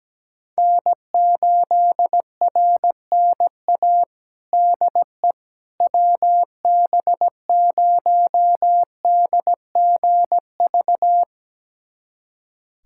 Morse code, heard as a keyed tone, about 17 wpm, 700 Hz. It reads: N8RNA DE WB0DGV
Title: USA Callsign DE Callsign 17wpm